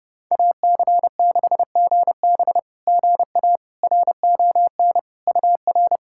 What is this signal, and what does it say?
Morse code, heard as a keyed tone, about 30 wpm, 690 Hz. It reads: AC6GB GA RON UR
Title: SOTA QSO Practice 30wpm